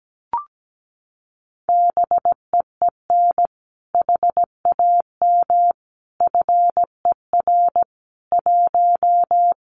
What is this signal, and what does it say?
Morse code, heard as a keyed tone, about 17 wpm, 700 Hz. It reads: BEEN HAM FER 1
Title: Common QSO Phrases Encore 17wpm